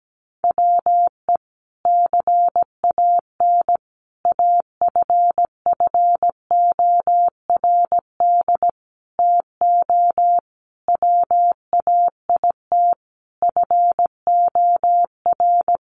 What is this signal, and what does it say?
Morse code, heard as a keyed tone, about 17 wpm, 700 Hz. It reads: WE CAN AFFORD TO WAIT FOR
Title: Sentences from Top 1100 Words 17wpm